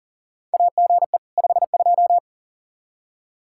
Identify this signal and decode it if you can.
Morse code, heard as a keyed tone, about 40 wpm, 700 Hz. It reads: AGE 52
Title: Practice QSO - Exchange 3 - More Info - 40wpm